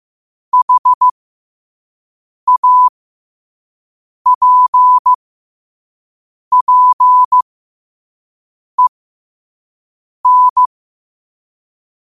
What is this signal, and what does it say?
Morse code, heard as a keyed tone, about 15 wpm, spaced as for 5 wpm, 1000 Hz. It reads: HAPPEN